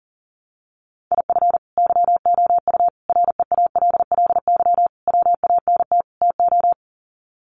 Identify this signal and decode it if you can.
Morse code, heard as a keyed tone, about 40 wpm, 700 Hz. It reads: IF YOU REALLY WANT TO
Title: Sets of 5 Words Encore2 40wpm